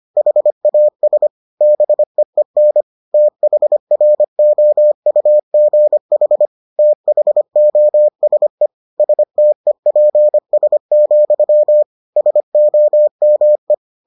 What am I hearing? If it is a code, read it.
Morse code, heard as a keyed tone, about 25 wpm, 600 Hz. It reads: HAS BEEN THROUGH THOSE STEPS, SOME